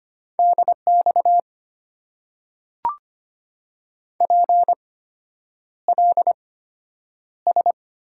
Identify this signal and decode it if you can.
Morse code, heard as a keyed tone, about 25 wpm, spaced as for 9 wpm, 690 Hz. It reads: DX PLS
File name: Top 90 QSO Elements - 8x Character Spacing - 25wpm